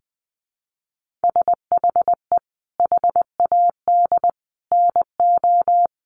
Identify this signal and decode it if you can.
Morse code, heard as a keyed tone, about 20 wpm, 700 Hz. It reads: SHE HAD NO
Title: Sentences from Top 1500 Words 20wpm